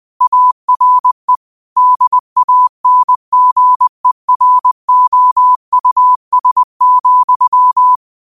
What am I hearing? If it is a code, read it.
Morse code, heard as a keyed tone, about 20 wpm, 1000 Hz. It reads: ARE DANGEROUS,